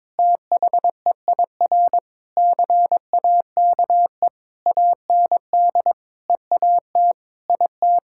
Morse code, heard as a keyed tone, about 22 wpm, 700 Hz. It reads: THEIR CAKE AND EAT IT
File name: practice / 22wpm / Sentences with 8-Word Phrases - 22wpm